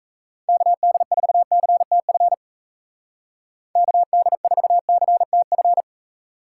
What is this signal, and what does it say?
Morse code, heard as a keyed tone, about 34 wpm, 700 Hz. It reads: KD4CTF KD4CTF
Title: Speed Racing - USA 2x3 Call Signs 28wpm